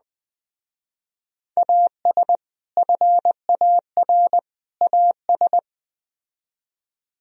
Morse code, heard as a keyed tone, about 20 wpm, 700 Hz. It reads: AS FAR AS
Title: Speed-Racing - Sets of 3 Words 17wpm